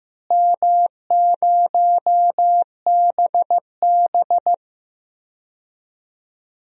Morse code, heard as a keyed tone, about 15 wpm, 700 Hz. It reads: M0BB
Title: DXCC-100 Call Signs - 4 Characters - Repeated 3x - 15wpm